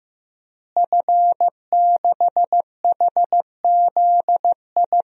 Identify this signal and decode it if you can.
Morse code, heard as a keyed tone, about 15 wpm, 700 Hz. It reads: F6HZI
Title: DXCC-100 Call Signs - 5 Characters - Repeated 3x - 15wpm